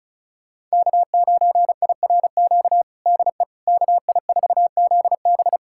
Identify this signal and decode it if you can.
Morse code, heard as a keyed tone, about 36 wpm, 700 Hz. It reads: K9IRQ DE KI4ZB
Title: USA Callsign DE Callsign - 5 Characters - 35wpm